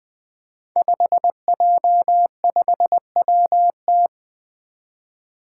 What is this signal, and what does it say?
Morse code, heard as a keyed tone, about 20 wpm, 700 Hz. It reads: 5J5WT
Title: Five Challenging Letter-Number Combinations 20wpm